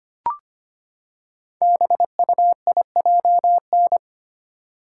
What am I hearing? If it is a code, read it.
Morse code, heard as a keyed tone, about 25 wpm, 690 Hz. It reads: BUIJN
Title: Code Groups - 5 Letters - 25wpm